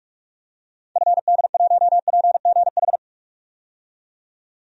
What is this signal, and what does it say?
Morse code, heard as a keyed tone, about 44 wpm, 700 Hz. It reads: FD1PGH
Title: DXCC-100 Call Signs - 6 Characters - 45wpm